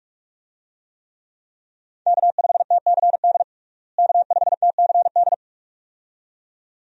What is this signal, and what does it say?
Morse code, heard as a keyed tone, about 42 wpm, 700 Hz. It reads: K5TCD K5TCD
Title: USA Call Signs - 5 Characters - Repeated 2x - 45wpm